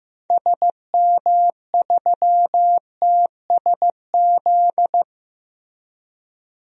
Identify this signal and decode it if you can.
Morse code, heard as a keyed tone, about 15 wpm, 700 Hz. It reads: SM3TSZ
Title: DXCC-100 Call Signs - 6 Characters - Repeated 2x - 15wpm